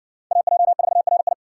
Morse code, heard as a keyed tone, about 60 wpm, 700 Hz. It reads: AJ4RI